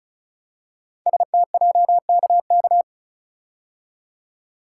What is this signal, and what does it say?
Morse code, heard as a keyed tone, about 36 wpm, 700 Hz. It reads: STJKK
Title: Code Groups - 5 Letters - 35wpm